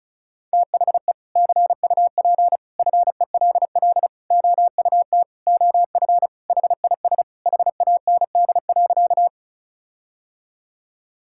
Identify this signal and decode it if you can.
Morse code, heard as a keyed tone, about 34 wpm, 700 Hz. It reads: THE CUP FELL OUT OF HIS HAND.
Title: Sentences from Top 1600 Words 35wpm